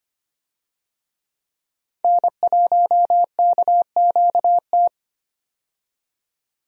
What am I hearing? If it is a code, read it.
Morse code, heard as a keyed tone, about 25 wpm, 700 Hz. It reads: N1KQT
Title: USA Call Signs - 5 Characters - 25wpm